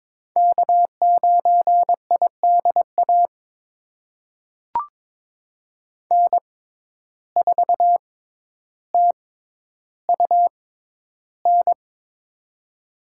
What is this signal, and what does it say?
Morse code, heard as a keyed tone, about 22 wpm, spaced as for 11 wpm, 700 Hz. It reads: K9IDA N4TUN
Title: USA Call Signs - 6x Character Spacing - 22wpm